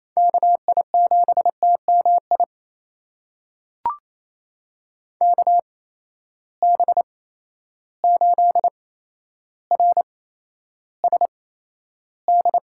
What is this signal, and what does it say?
Morse code, heard as a keyed tone, about 28 wpm, spaced as for 12 wpm, 700 Hz. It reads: KI7TMI KB8RSD
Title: USA Call Signs - 6 Characters - 8x Character Spacing - 28wpm